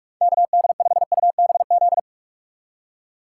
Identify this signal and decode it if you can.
Morse code, heard as a keyed tone, about 46 wpm, 700 Hz. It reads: KD5UBZ
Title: USA Call Signs - 6 Characters - 2x Character Spacing - 45wpm